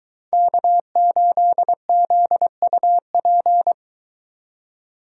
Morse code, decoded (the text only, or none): K8ZUP